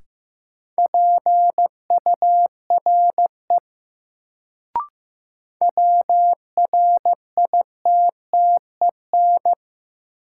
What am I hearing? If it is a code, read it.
Morse code, heard as a keyed tone, about 15 wpm, 700 Hz. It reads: PURE WRITTEN
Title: Sentences from Top 1500 Words - New Word Drill - 15wpm